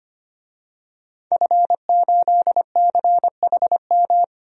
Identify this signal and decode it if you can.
Morse code, heard as a keyed tone, about 25 wpm, 700 Hz. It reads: F8CHM
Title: DXCC-100 Call Signs - 5 Characters - 25wpm